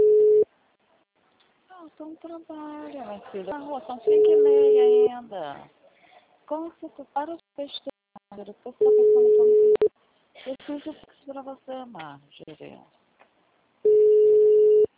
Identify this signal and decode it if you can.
Ringback tone (Europe, 425 Hz), with speech.